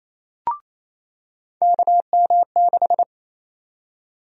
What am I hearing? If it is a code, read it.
Morse code, heard as a keyed tone, about 28 wpm, 700 Hz. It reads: KM6
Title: USA Call Sign Prefixes - 28wpm